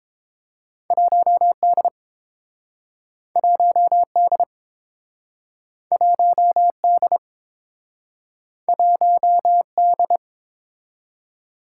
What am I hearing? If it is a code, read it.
Morse code, heard as a keyed tone, about 30 wpm, 700 Hz. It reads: 1D 1D 1D 1D